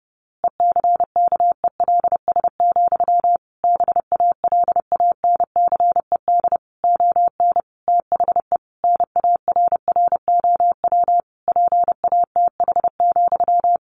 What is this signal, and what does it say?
Morse code, heard as a keyed tone, about 30 wpm, 700 Hz. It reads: ECKELS, BALANCED ON THE NARROW PATH,
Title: A Sound Of Thunder 30wpm